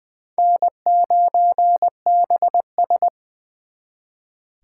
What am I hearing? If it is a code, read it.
Morse code, heard as a keyed tone, about 23 wpm, 700 Hz. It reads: N9BS
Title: Speed Racing - USA 1x2 Call Signs 20wpm